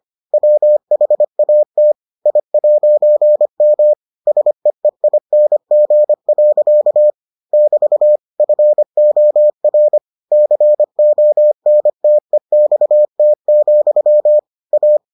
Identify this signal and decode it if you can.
Morse code, heard as a keyed tone, about 25 wpm, 600 Hz. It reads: WHAT I'M SEEING. <BT> FOR CONTEXT, A